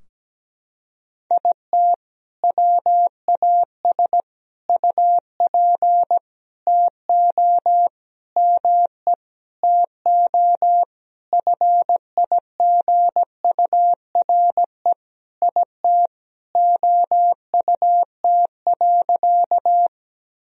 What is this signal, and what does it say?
Morse code, heard as a keyed tone, about 17 wpm, 700 Hz. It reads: IT WAS UP TO ME TO FIGURE IT OUT.
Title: Sentences from Top 1000 Words - Review Part 2 - 17wpm